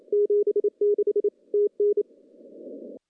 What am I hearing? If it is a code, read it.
Morse code, heard as a keyed tone, about 28 wpm, 410 Hz. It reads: 76 TN
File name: contests / CW-SS-k5tr- -0200